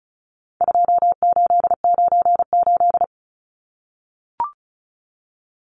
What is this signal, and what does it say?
Morse code, heard as a keyed tone, about 34 wpm, 700 Hz. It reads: 2898
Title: Numbers - 4-Digit 35wpm